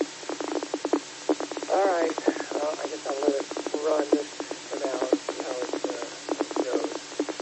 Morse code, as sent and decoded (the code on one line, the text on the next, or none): none